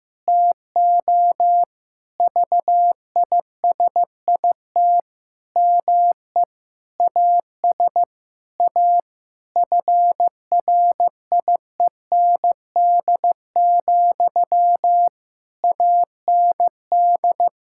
Morse code, heard as a keyed tone, about 15 wpm, 700 Hz. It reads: TO VISIT ME AS A FRIEND, AND
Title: Sentences from Top 1000 Words - Review Part 6 - 15wpm